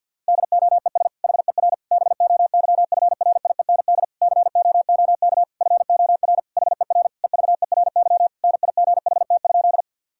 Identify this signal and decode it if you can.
Morse code, heard as a keyed tone, about 52 wpm, 700 Hz. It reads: DOES HER BOYFRIEND COOK FOR HER EVERY NIGHT?
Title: Sentences from Top 1000 Words - Review Part 5 - 50wpm